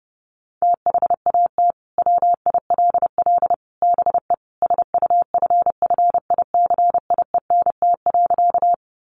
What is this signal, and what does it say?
Morse code, heard as a keyed tone, about 28 wpm, 700 Hz. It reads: THAT WILL BE SUFFICIENT.
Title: Sentences from Top 1400 Words 30wpm